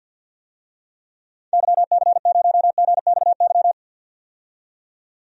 Morse code, heard as a keyed tone, about 48 wpm, 700 Hz. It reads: YC0GXY
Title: DXCC-100 Call Signs - 6 Characters - 50wpm